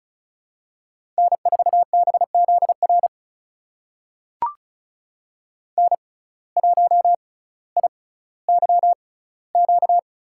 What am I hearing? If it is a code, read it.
Morse code, heard as a keyed tone, about 36 wpm, spaced as for 18 wpm, 700 Hz. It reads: N4BZR N1IYQ
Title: USA Call Signs - 5 Characters - 6x Character Spacing - 35wpm